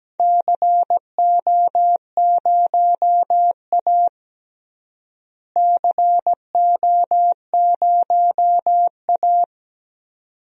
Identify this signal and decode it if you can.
Morse code, heard as a keyed tone, about 17 wpm, 700 Hz. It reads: CO0A CO0A
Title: DXCC-100 Call Signs - 4 Characters - Repeated 2x - 17wpm